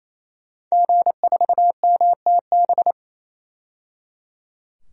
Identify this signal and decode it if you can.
Morse code, heard as a keyed tone, about 28 wpm, 700 Hz. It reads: G4MTB
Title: DXCC-100 Call Signs - 5 Characters - 28wpm